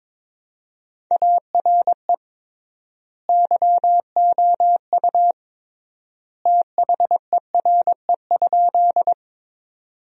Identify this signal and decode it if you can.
Morse code, heard as a keyed tone, about 21 wpm, 700 Hz. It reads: ARE YOU THERE?
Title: 3x Word-Spacing - Sentences from Top 100 Words - 22wpm